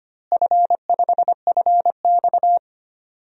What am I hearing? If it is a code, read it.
Morse code, heard as a keyed tone, about 25 wpm, 700 Hz. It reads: F5FX
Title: DXCC-100 Call Signs - 4 Characters - Repeated 3x - 25wpm